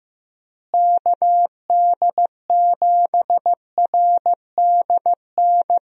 Morse code, heard as a keyed tone, about 15 wpm, 700 Hz. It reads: KD7RDN
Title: USA Call Signs - 6 Characters - Repeated 3x - 15wpm